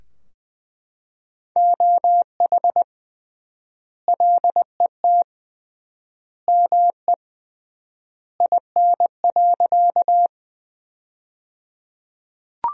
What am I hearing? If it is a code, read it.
Morse code, heard as a keyed tone, about 20 wpm, 700 Hz. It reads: OH LET ME IN.
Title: 3x Word-Spacing - Sentences from Top 500 Words - 20wpm